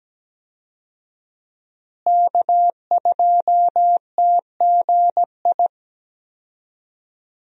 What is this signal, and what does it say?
Morse code, heard as a keyed tone, about 17 wpm, 700 Hz. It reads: K2TGI
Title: USA Call Signs - 5 Characters - Repeated 3x - 17wpm